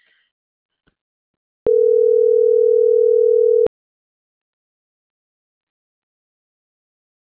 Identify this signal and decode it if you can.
Ringback tone (North America), 440 + 480 Hz.